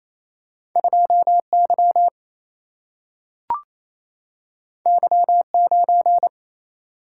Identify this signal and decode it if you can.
Morse code, heard as a keyed tone, about 28 wpm, 700 Hz. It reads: 2 Y Y 9
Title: Single-Letter Single-Number 28wpm